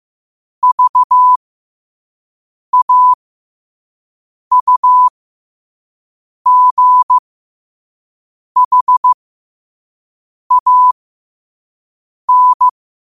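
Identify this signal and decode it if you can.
Morse code, heard as a keyed tone, about 15 wpm, spaced as for 5 wpm, 1000 Hz. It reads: VAUGHAN
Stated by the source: Morse code sent at 5 wpm.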